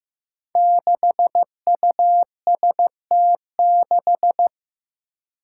Morse code, heard as a keyed tone, about 16 wpm, 700 Hz. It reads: 6UST6
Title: Five Challenging Letter-Number Combinations 15wpm